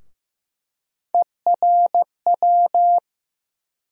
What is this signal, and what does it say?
Morse code, heard as a keyed tone, about 15 wpm, 700 Hz. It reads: ERW
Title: USA Call Sign Postfixes - 3 Letters 15wpm